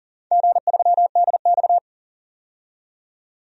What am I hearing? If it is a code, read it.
Morse code, heard as a keyed tone, about 40 wpm, 700 Hz. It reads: G3DX
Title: DXCC-100 Call Signs - 4 Characters - 40wpm